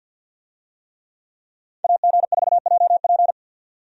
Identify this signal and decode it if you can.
Morse code, heard as a keyed tone, about 52 wpm, 700 Hz. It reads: AG4JP
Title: USA Call Signs - 5 Characters - 50wpm